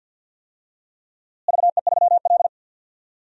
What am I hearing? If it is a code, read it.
Morse code, heard as a keyed tone, about 47 wpm, 700 Hz. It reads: VE3L
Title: DXCC-100 Call Signs - 4 Characters - Repeated 3x - 50wpm